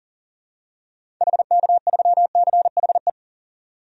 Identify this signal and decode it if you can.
Morse code, heard as a keyed tone, about 40 wpm, 700 Hz. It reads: HK3CHE
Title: DXCC-100 Call Signs - 6 Characters - 40wpm